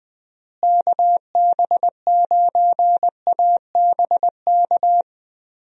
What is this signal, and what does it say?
Morse code, heard as a keyed tone, about 20 wpm, 700 Hz. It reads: KB9ABK
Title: USA Call Signs - 6 Characters - Repeated 3x - 20wpm